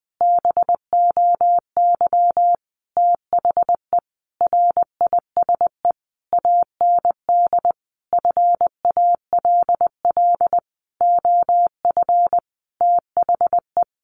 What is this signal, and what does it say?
Morse code, heard as a keyed tone, about 20 wpm, 700 Hz. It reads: BOY THE RISE AND FALL OF THE
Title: Sentences from Top 1500 Words 20wpm